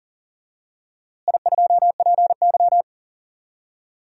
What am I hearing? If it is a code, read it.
Morse code, heard as a keyed tone, about 40 wpm, 700 Hz. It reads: I2PY
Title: DXCC-100 Call Signs - 4 Characters - Repeated 3x - 40wpm